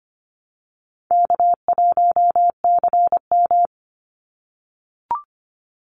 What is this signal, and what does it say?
Morse code, heard as a keyed tone, about 25 wpm, 700 Hz. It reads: K1CM
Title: NA SOTA Chaser Callsigns 25wpm